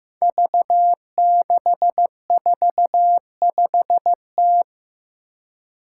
Morse code, heard as a keyed tone, about 15 wpm, 700 Hz. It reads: V645T
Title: Five Challenging Letter-Number Combinations 15wpm